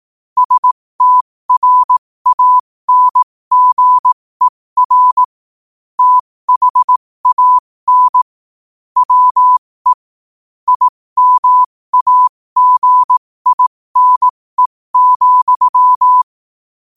Morse code, heard as a keyed tone, about 18 wpm, spaced as for 15 wpm, 1000 Hz. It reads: STRANGER THAN WE IMAGINE,
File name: qotdfiles / 15wpm